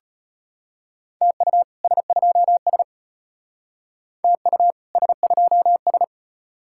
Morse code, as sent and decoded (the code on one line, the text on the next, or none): - ..- / ... ..--- ... / - ..- / ... ..--- ...
TU S2S TU S2S